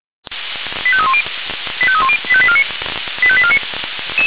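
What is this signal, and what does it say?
Five-tone selective call (ZVEI), heard as quick runs of five tones, tones of about 70 ms. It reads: 85310 85310 85849 85849
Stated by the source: ZVEI five-tone call received on SDR, Feuerwehr Orlamünde.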